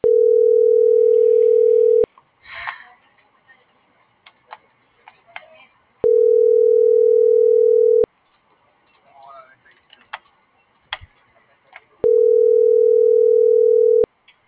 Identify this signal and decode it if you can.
Ringback tone (North America), 440 + 480 Hz.
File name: CENTRAL GENERAL / out karla.sky